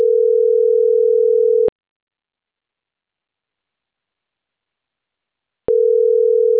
Ringback tone (North America), 440 + 480 Hz.
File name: CENTRAL CAEA / out agcaea10